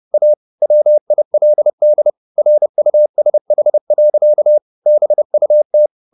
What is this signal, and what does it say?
Morse code, heard as a keyed tone, about 30 wpm, 600 Hz. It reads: A WILD RUSH. BUT
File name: the old man and the sea / the old man and the sea 1 30wpm